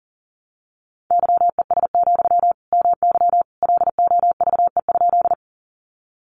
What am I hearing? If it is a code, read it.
Morse code, heard as a keyed tone, about 40 wpm, 700 Hz. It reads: YES, MY LOVE?
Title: Sentences from Top 500 Words 40wpm